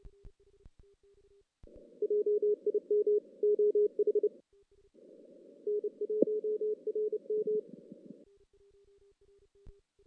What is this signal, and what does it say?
Morse code, heard as a keyed tone, about 30 wpm, 410 Hz. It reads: JIM OH N1RM
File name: 2016 JAN CW NAQP